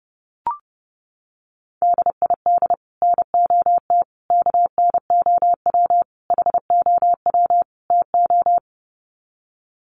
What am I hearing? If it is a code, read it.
Morse code, heard as a keyed tone, about 30 wpm, 700 Hz. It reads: DID NOT KNOW HOW TO
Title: Sets of 5 Words 30wpm